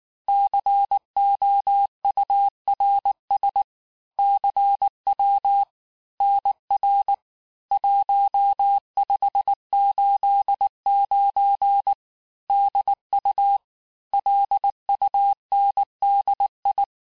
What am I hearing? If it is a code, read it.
Morse code, heard as a keyed tone, about 19 wpm, 800 Hz. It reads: COURS CW NR 1589 DU LUNDI
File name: cours-du-lundi files / cours-1589-texte-chiffre-vitesse-18mpm